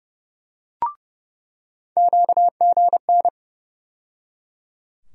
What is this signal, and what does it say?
Morse code, heard as a keyed tone, about 30 wpm, 700 Hz. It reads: QGN
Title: USA Call Sign Postfixes - 3 Letters 30wpm